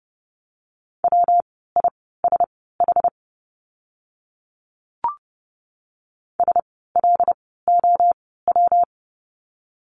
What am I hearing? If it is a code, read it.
Morse code, heard as a keyed tone, about 30 wpm, spaced as for 17 wpm, 700 Hz. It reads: WISH SLOW
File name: course / 30:10wpm / 115 Lesson 19f - Letter w - Words 30wpm